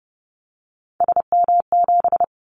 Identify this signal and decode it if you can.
Morse code, heard as a keyed tone, about 30 wpm, 700 Hz. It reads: SM7
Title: Callsign Prefixes for Top 25 DXCC - 30wpm